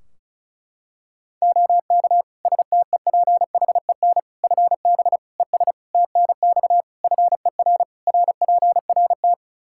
Morse code, heard as a keyed tone, about 34 wpm, 700 Hz. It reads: OK STEPHEN FB ES TNX FER RPRT
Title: Practice QSO - Exchange 2 - Info - 35wpm